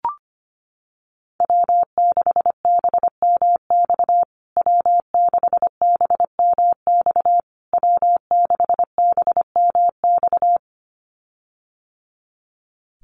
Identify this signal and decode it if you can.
Morse code, heard as a keyed tone, about 25 wpm, 700 Hz. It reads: W6BMX W6BMX W6BMX
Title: USA Call Signs Repeated 3x 25wpm